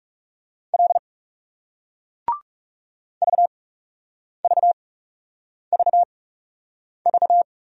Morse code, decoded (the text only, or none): L VVVV